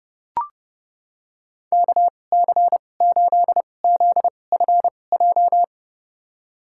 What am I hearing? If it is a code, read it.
Morse code, heard as a keyed tone, about 30 wpm, spaced as for 22 wpm, 700 Hz. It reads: KC8ZFJ
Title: USA Call Signs - 2x Character Spacing - 30wpm